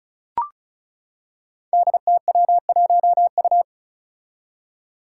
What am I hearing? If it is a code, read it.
Morse code, heard as a keyed tone, about 36 wpm, 700 Hz. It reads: DTW1U
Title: Five Challenging Letter-Number Combinations 35wpm